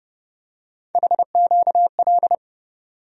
Morse code, heard as a keyed tone, about 30 wpm, 700 Hz. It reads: HQL
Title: USA Call Sign Postfixes - 3 Letters 30wpm